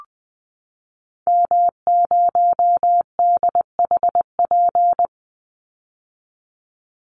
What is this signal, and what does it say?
Morse code, heard as a keyed tone, about 20 wpm, 700 Hz. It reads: M0DHP
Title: Call Signs - 20wpm